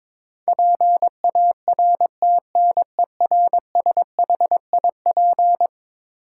Morse code, heard as a keyed tone, about 22 wpm, 700 Hz. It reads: PARTNERSHIP
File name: practice / 22wpm / Eleven Letter Words - 22wpm